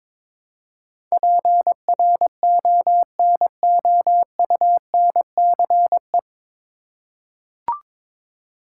Morse code, heard as a keyed tone, about 22 wpm, 700 Hz. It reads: PRONOUNCE
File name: 9 Letter Words - First 3 Letters Given - 22wpm